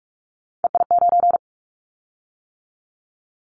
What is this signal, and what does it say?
Morse code, heard as a keyed tone, about 46 wpm, 700 Hz. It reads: EI9
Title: Callsign Prefixes for Top 50 DXCC - 45wpm